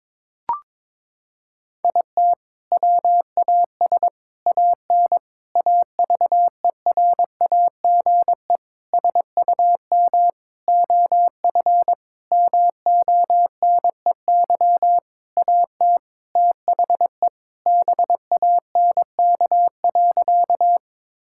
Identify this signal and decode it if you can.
Morse code, heard as a keyed tone, about 22 wpm, 700 Hz. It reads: IT WAS AN AVERAGE SUM OF MONEY AT THE BANK.
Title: Sentences from Top 1000 Words - Review Part 6 - 22wpm